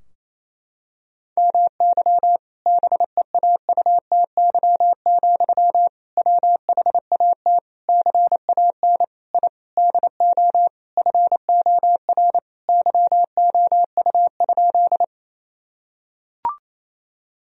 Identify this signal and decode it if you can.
Morse code, heard as a keyed tone, about 28 wpm, 700 Hz. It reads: MY BEAUTY, WHAT CAN I DO FOR YOU?
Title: Sentences from Top 1000 Words - Review Part 2 - 28wpm